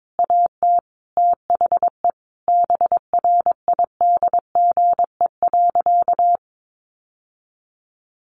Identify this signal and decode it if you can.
Morse code, heard as a keyed tone, about 22 wpm, 700 Hz. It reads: AT THE BRIDGE.
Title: Sentences from Top 1200 Words 22wpm